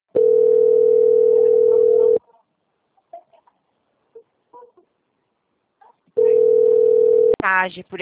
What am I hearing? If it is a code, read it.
Ringback tone (North America), 440 + 480 Hz, with speech.
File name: out- - - - - .28288